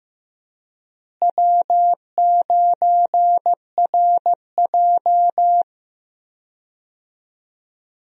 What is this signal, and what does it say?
Morse code, heard as a keyed tone, about 15 wpm, 700 Hz. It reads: W9RJ
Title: USA Call Signs - 4 Characters - Repeated 3x - 15wpm